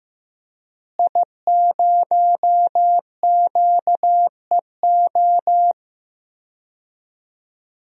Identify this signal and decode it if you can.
Morse code, heard as a keyed tone, about 15 wpm, 700 Hz. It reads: I0QEO
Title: DXCC-100 Call Signs - 5 Characters - Repeated 3x - 15wpm